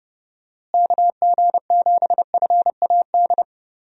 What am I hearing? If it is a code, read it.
Morse code, heard as a keyed tone, about 30 wpm, 700 Hz. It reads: KG7FAD
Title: USA Call Signs - 6 Characters - Repeated 2x - 30wpm